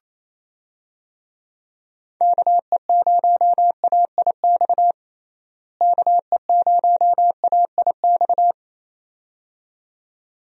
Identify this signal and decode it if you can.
Morse code, heard as a keyed tone, about 28 wpm, 700 Hz. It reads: KE0AIX KE0AIX
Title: USA Call Signs - 6 Characters - Repeated 2x - 28wpm